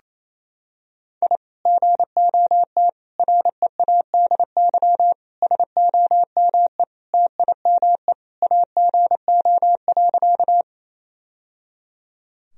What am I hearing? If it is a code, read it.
Morse code, heard as a keyed tone, about 28 wpm, 700 Hz. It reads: I GOT READY SOME TIME AGO.
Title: Sentences from Top 1000 Words - Review Part 3 - 28wpm